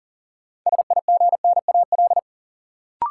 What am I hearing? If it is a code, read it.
Morse code, heard as a keyed tone, about 40 wpm, 700 Hz. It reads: SIGNAL
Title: Sentences from Top 1400 Words - New Word Drill - 40wpm